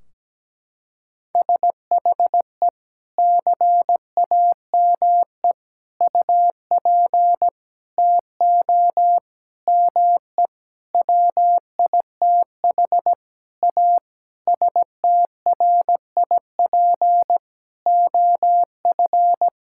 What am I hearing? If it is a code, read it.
Morse code, heard as a keyed tone, about 17 wpm, 700 Hz. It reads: SHE CAME UP TO ME WITH A STRIP OF
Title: Sentences from Top 1600 Words 17wpm